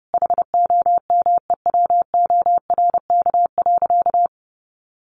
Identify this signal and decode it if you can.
Morse code, heard as a keyed tone, about 30 wpm, 700 Hz. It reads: HOMEWORK.
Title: Additional Easy Phrases 30wpm